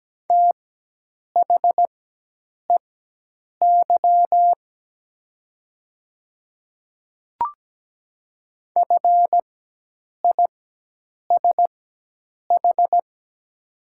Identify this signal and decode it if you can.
Morse code, heard as a keyed tone, about 17 wpm, spaced as for 8 wpm, 700 Hz. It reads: THEY FISH
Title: 4x Character Spacing - Top 300 Words - 17wpm